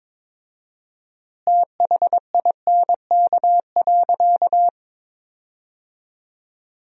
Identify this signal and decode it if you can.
Morse code, heard as a keyed tone, about 23 wpm, 700 Hz. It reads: THINK.
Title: 4x Word-Spacing - Sentences from Top 400 Words - 22wpm